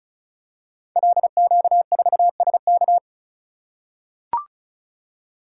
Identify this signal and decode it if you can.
Morse code, heard as a keyed tone, about 36 wpm, 700 Hz. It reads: LQ4SK
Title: Code Groups - 5 Letter:Numbers - 35wpm